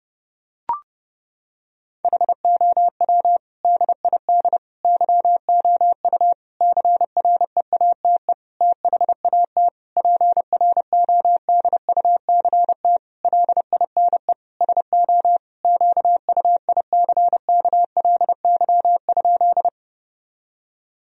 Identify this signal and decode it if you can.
Morse code, heard as a keyed tone, about 30 wpm, 700 Hz. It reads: HOW DID YOU CREATE THAT PRODUCT LINE SO QUICKLY?
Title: Sentences from Top 1000 Words - Review Part 3 - 30wpm